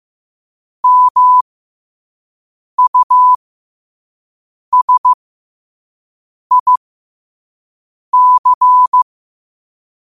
Morse code, heard as a keyed tone, about 15 wpm, spaced as for 5 wpm, 1000 Hz. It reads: MUSIC